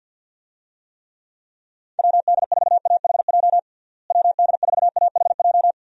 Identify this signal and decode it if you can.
Morse code, heard as a keyed tone, about 53 wpm, 700 Hz. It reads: WD4AHJ WD4AHJ
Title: USA Call Signs - 6 Characters - Repeated 2x - 50wpm